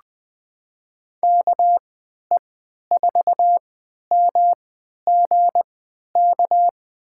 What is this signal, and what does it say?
Morse code, heard as a keyed tone, about 20 wpm, spaced as for 11 wpm, 700 Hz. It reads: KE4MGK
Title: USA Call Signs - 6 Characters - 3x Character Spacing - 20wpm